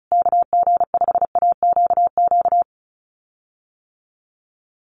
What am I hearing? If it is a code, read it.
Morse code, heard as a keyed tone, about 34 wpm, 700 Hz. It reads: KG5AQQ
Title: USA Call Signs Repeated 3x 35wpm